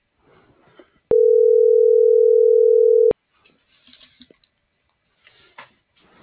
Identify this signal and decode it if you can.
Ringback tone (North America), 440 + 480 Hz.